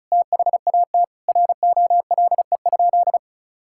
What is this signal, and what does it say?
Morse code, heard as a keyed tone, about 34 wpm, 700 Hz. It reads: THAT ROLE?
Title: Sentences from Top 1000 Words - Review Part 6 - 35wpm